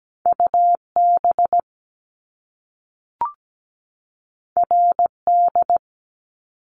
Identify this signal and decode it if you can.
Morse code, heard as a keyed tone, about 17 wpm, 700 Hz. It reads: UB RD
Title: Top Two Letter Combinations in Words 17wpm